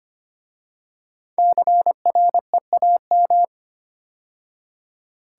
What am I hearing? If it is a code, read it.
Morse code, heard as a keyed tone, about 26 wpm, 700 Hz. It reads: CREAM
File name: 5 Letter Words - First 3 Letters Given - 25wpm